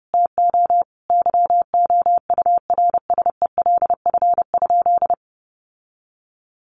Morse code, heard as a keyed tone, about 30 wpm, 700 Hz. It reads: TO YOURSELF?
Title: Sentences from Top 1200 Words 30wpm